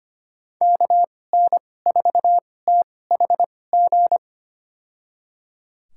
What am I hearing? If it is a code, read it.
Morse code, heard as a keyed tone, about 25 wpm, spaced as for 18 wpm, 700 Hz. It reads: KN4THG
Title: USA Call Signs - 6 Characters - 2x Character Spacing - 25wpm